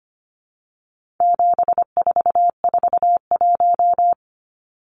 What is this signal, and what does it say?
Morse code, heard as a keyed tone, about 25 wpm, 700 Hz. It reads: 7441
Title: Numbers - 4-Digit 25wpm